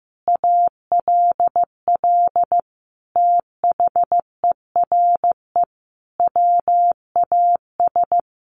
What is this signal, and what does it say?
Morse code, heard as a keyed tone, about 15 wpm, 700 Hz. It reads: ALL THERE WAS
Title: Sentences from Top 1100 Words 15wpm